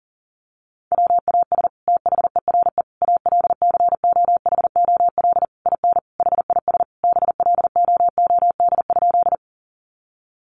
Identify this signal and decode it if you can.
Morse code, heard as a keyed tone, about 40 wpm, 700 Hz. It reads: WAS THERE ALCOHOL IN HIS BLOOD?
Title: Sentences from Top 1400 Words 40wpm